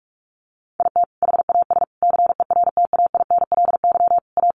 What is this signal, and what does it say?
Morse code, heard as a keyed tone, about 44 wpm, 700 Hz. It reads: IT HAS CERTAINLY A
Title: Sentences from Top 1200 Words 45wpm